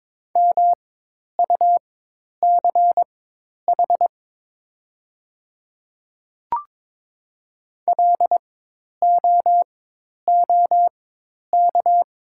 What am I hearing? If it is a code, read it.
Morse code, heard as a keyed tone, about 22 wpm, spaced as for 10 wpm, 700 Hz. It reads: MUCH LOOK
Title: 4x Character Spacing - Top 300 Words - 22wpm